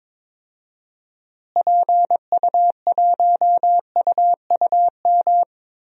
Morse code, heard as a keyed tone, about 22 wpm, 700 Hz. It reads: PU1UUM
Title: DXCC-100 Call Signs - 6 Characters - Repeated 2x - 22wpm